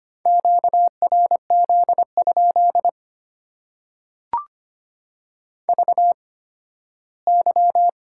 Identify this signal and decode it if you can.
Morse code, heard as a keyed tone, about 25 wpm, spaced as for 15 wpm, 700 Hz. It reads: QRZ? VY
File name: Top 90 QSO Elements - 8x Character Spacing - 25wpm